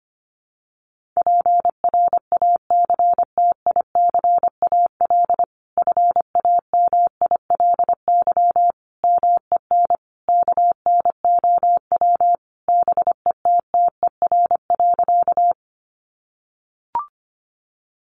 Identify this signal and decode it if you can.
Morse code, heard as a keyed tone, about 25 wpm, 700 Hz. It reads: PRACTICAL FAMILY MEN KNOW BETTER.
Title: Sentences from Top 1400 Words 25wpm